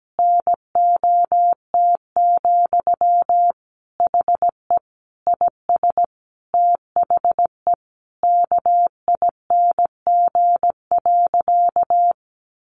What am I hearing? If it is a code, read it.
Morse code, heard as a keyed tone, about 17 wpm, 700 Hz. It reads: NOT, HE IS THE KING.
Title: Sentences from Top 1200 Words 17wpm